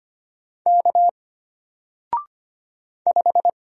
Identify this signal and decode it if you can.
Morse code, heard as a keyed tone, about 25 wpm, 700 Hz. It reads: K 5
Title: Single Letter-Number 25wpm